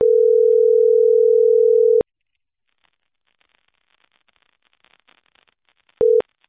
Ringback tone (North America), 440 + 480 Hz.